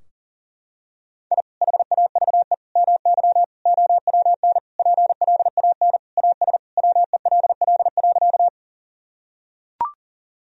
Morse code, heard as a keyed tone, about 37 wpm, 700 Hz. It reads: I HAVE MY OWN PLAN AS WELL.
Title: Sentences from Top 1000 Words - Review Part 2 - 40wpm